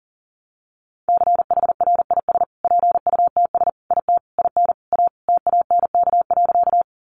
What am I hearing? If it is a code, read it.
Morse code, heard as a keyed tone, about 40 wpm, 700 Hz. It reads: CHRIS PUTS IT IN A TANK.
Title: Easy Phrases 40wpm